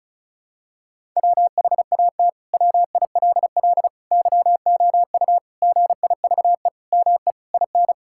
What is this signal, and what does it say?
Morse code, heard as a keyed tone, about 34 wpm, 700 Hz. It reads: WHAT WILL YOU GIVE ME IN
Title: Sentences from Top 1000 Words - Review Part 6 - 35wpm